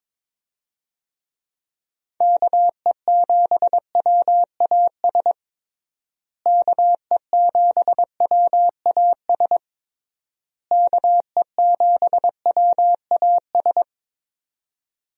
Morse code, heard as a keyed tone, about 22 wpm, 700 Hz. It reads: KE7WAS KE7WAS KE7WAS
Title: USA Call Signs - 6 Characters - Repeated 3x - 22wpm